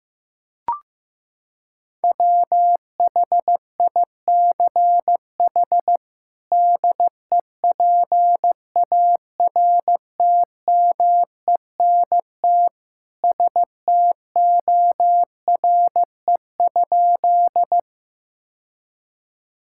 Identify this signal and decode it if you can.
Morse code, heard as a keyed tone, about 16 wpm, 700 Hz. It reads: WHICH DEPARTMENT STORE?
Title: Sentences from Top 1000 Words - Review Part 5 - 15wpm